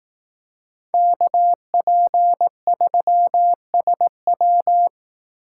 Morse code, heard as a keyed tone, about 18 wpm, 700 Hz. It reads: KP3SW
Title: Speed Racing - USA Call Signs 15wpm